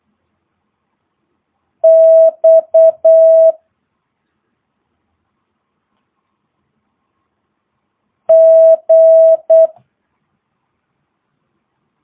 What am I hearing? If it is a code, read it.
Morse code, heard as a keyed tone, about 8 wpm, 650 Hz. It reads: X G